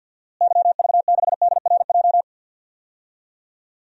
Morse code, heard as a keyed tone, about 52 wpm, 700 Hz. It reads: YV6DRJ
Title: DXCC-100 Call Signs - 6 Characters - Repeated 2x - 50wpm